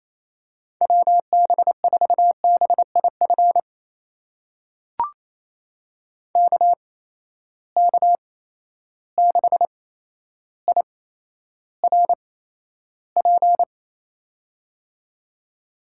Morse code, heard as a keyed tone, about 28 wpm, spaced as for 12 wpm, 700 Hz. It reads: WB4BIF KK6IRP
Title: USA Call Signs - 6 Characters - 8x Character Spacing - 28wpm